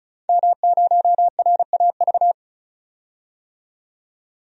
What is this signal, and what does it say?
Morse code, heard as a keyed tone, about 36 wpm, 700 Hz. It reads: M0RAV
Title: DXCC-100 Call Signs - 5 Characters - 35wpm